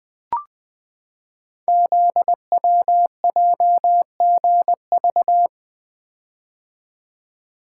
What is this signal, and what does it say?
Morse code, heard as a keyed tone, about 20 wpm, 700 Hz. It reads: ZWJGV
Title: Code Groups - 5 Letters - 20wpm